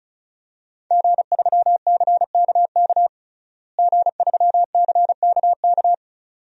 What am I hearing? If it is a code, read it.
Morse code, heard as a keyed tone, about 34 wpm, 700 Hz. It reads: G3CKK G3CKK
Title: DXCC-100 Call Signs - 5 Characters - Repeated 2x - 35wpm